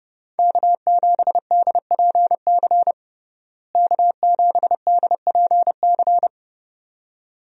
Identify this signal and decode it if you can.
Morse code, heard as a keyed tone, about 30 wpm, 700 Hz. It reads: K7DPC K7DPC
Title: USA Call Signs - 5 Characters - Repeated 2x - 30wpm